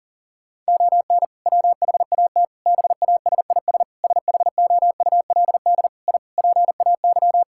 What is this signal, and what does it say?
Morse code, heard as a keyed tone, about 40 wpm, 700 Hz. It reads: ON WHAT BASIS SHOULD I PAY
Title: Sentences from Top 1000 Words - Review Part 5 - 40wpm